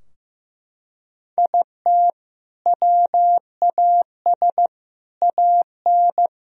Morse code, heard as a keyed tone, about 15 wpm, 700 Hz. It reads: IT WAS AN
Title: Sentences from Top 1000 Words - Review Part 4 - 15wpm